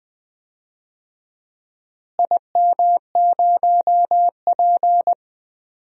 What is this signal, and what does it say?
Morse code, heard as a keyed tone, about 20 wpm, 700 Hz. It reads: IM0P
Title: DXCC-100 Call Signs - 4 Characters - Repeated 3x - 20wpm